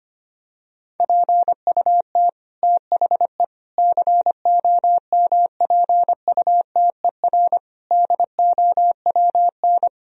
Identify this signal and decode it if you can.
Morse code, heard as a keyed tone, about 25 wpm, 700 Hz. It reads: PUT THE COMPUTER DOWN
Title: Sentences from Top 1000 Words - Review Part 3 - 25wpm